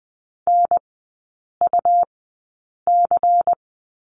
Morse code, heard as a keyed tone, about 20 wpm, spaced as for 8 wpm, 700 Hz. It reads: NUC
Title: 191 Lesson 31e - Letter v - Three Characters 20wpm